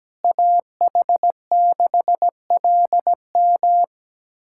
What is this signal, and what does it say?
Morse code, heard as a keyed tone, about 17 wpm, 700 Hz. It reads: AH6LM
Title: USA Callsign DE Callsign - 5 Characters - 17wpm